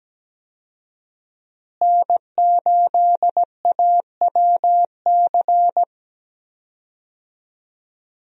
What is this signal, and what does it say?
Morse code, heard as a keyed tone, about 17 wpm, 700 Hz. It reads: N8AWC
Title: USA Call Signs - 5 Characters - Repeated 2x - 17wpm